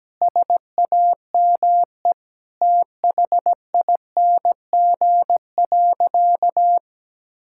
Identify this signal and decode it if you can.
Morse code, heard as a keyed tone, about 17 wpm, 700 Hz. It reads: SAME THING.
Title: Sentences from Top 1000 Words - Review Part 6 - 17wpm